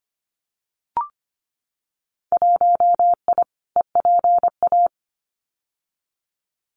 Morse code, heard as a keyed tone, about 25 wpm, 700 Hz. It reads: 1I EPA
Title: Winter Field Day Exchanges Encore 25wpm